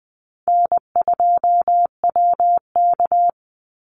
Morse code, heard as a keyed tone, about 19 wpm, 700 Hz. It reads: N2WK
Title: Easy Call Signs Repeated 2x - 20wpm